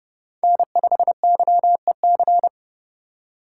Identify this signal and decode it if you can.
Morse code, heard as a keyed tone, about 30 wpm, 700 Hz. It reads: N5YEC